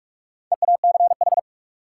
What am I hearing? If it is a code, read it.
Morse code, heard as a keyed tone, about 46 wpm, 700 Hz. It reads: EACH
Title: Sentences from Top 200 Words - New Word Drill - 45wpm